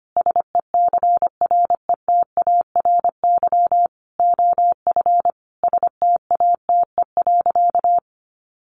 Morse code, heard as a keyed tone, about 25 wpm, 700 Hz. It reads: SECRETARY OF STATE.
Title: Sentences from Top 1400 Words 25wpm